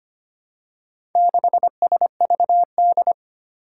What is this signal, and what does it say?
Morse code, heard as a keyed tone, about 25 wpm, 700 Hz. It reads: 6SVD
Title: Four Challenging Letter-Number Combinations 25wpm